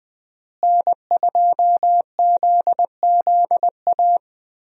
Morse code, heard as a keyed tone, about 20 wpm, 700 Hz. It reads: N2ZZA
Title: USA Call Signs - 5 Characters - Repeated 2x - 20wpm